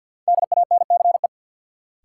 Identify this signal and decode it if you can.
Morse code, heard as a keyed tone, about 52 wpm, 700 Hz. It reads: DANCE
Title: Sentences from Top 700 Words - New Word Drill - 50wpm